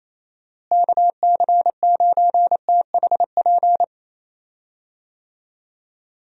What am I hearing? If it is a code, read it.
Morse code, heard as a keyed tone, about 27 wpm, 700 Hz. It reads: KC9THP
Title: USA Call Signs - 6 Characters - Repeated 2x - 28wpm